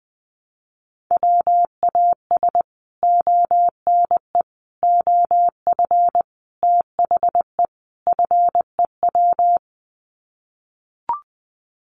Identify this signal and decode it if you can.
Morse code, heard as a keyed tone, about 20 wpm, 700 Hz. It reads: WAS ONE OF THE FEW
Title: Sets of 5 Words Encore 20wpm